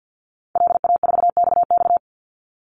Morse code, heard as a keyed tone, about 50 wpm, 700 Hz. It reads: LA4XX
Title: Call Signs - 50wpm